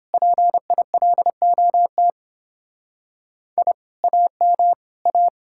Morse code, heard as a keyed tone, about 29 wpm, 700 Hz. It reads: PILOT I AM A